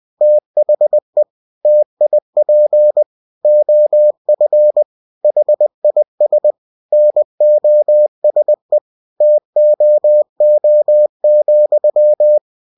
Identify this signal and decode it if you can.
Morse code, heard as a keyed tone, about 20 wpm, 600 Hz. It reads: THE TIP OF HIS NOSE TOO,
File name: cw-books / winnie-the-pooh / winnie-20wpm-page-0024